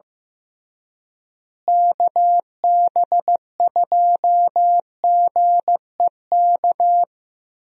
Morse code, heard as a keyed tone, about 15 wpm, 700 Hz. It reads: KB2GEK
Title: USA Call Signs - 6 Characters - Repeated 2x - 15wpm